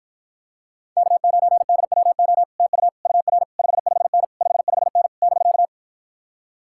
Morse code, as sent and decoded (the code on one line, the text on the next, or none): -.- ----. -.. .-- --- / - ..- / ..- .-. / ..... ..... -. / ..... ..... -. / -...-.-
K9DWO TU UR 55N 55N <BK>